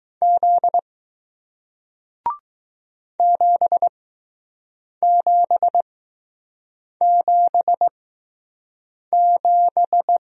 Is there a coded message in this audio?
Yes: Morse code from 0.2 to 10.2 s, heard as a keyed tone, about 20 wpm, spaced as for 6 wpm, 700 Hz. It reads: Z 7777